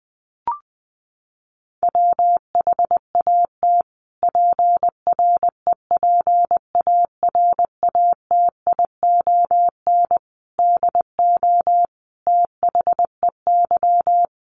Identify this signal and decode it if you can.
Morse code, heard as a keyed tone, about 20 wpm, 700 Hz. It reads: WHAT PREPARATION DO THEY
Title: Sentences from Top 1400 Words 20wpm